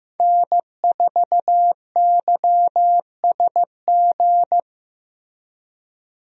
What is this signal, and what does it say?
Morse code, heard as a keyed tone, about 15 wpm, 700 Hz. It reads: N4YSG
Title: USA Call Signs - 5 Characters - Repeated 3x - 15wpm